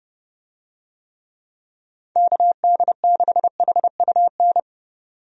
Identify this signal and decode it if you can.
Morse code, heard as a keyed tone, about 30 wpm, 700 Hz. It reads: KD6HUN
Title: USA Call Signs - 6 Characters - 30wpm